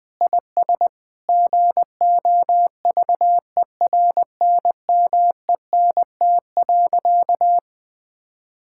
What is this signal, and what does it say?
Morse code, heard as a keyed tone, about 20 wpm, 700 Hz. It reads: IS GOVERNMENT.
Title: Sentences from Top 1000 Words - Review Part 6 - 20wpm